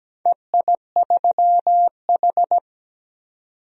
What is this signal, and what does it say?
Morse code, heard as a keyed tone, about 17 wpm, 700 Hz. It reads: EI3H
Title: DXCC-100 Call Signs - 4 Characters - 17wpm